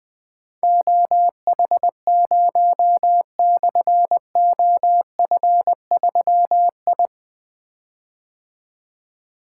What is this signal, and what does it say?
Morse code, heard as a keyed tone, about 20 wpm, 700 Hz. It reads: OH0/OF3I
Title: Difficult Call Signs 20wpm